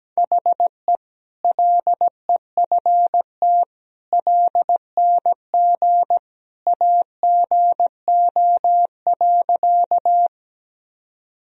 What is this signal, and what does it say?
Morse code, heard as a keyed tone, about 17 wpm, 700 Hz. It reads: HE LEFT LNG AGO.
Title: Sentences from Top 500 Words and Abbreviations 17wpm